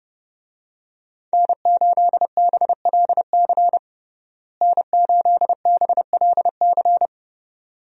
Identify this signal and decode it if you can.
Morse code, heard as a keyed tone, about 30 wpm, 700 Hz. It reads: N8BLC N8BLC
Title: USA Call Signs - 5 Characters - Repeated 2x - 30wpm